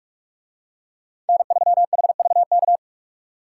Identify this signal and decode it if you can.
Morse code, heard as a keyed tone, about 46 wpm, 700 Hz. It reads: N3HVK